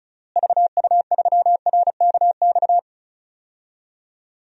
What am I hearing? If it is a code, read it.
Morse code, heard as a keyed tone, about 34 wpm, 700 Hz. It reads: VU3RKX
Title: DXCC-100 Call Signs - 6 Characters - Repeated 3x - 35wpm